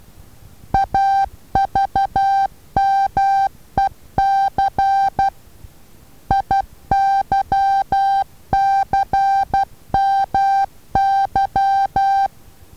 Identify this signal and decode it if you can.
Morse code, heard as a keyed tone, about 12 wpm, 790 Hz. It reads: AVMEC IYCMY
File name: Taviro HA6VA B